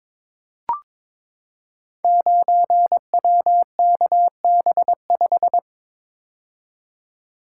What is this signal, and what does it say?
Morse code, heard as a keyed tone, about 22 wpm, 700 Hz. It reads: 9WKB5
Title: Code Groups - 5 Letter:Numbers - 22wpm